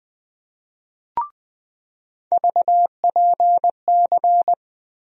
Morse code, heard as a keyed tone, about 20 wpm, 700 Hz. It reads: VPC
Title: USA Call Sign Postfixes - 3 Letters 20wpm